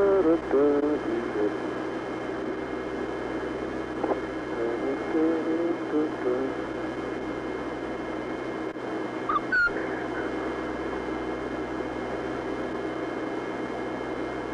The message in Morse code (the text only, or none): none